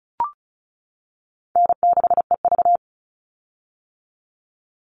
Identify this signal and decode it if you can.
Morse code, heard as a keyed tone, about 34 wpm, 700 Hz. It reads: N6EV
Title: Call Signs - 35wpm